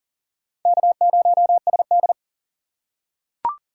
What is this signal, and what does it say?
Morse code, heard as a keyed tone, about 40 wpm, 700 Hz. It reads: K0SD
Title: USA Call Signs - 4 Characters - 3x Character Spacing - 40wpm